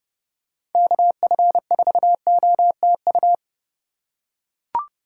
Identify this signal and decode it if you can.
Morse code, heard as a keyed tone, about 32 wpm, 700 Hz. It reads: KF4OTU
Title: USA Call Signs - 6 Characters - 4x Character Spacing - 30wpm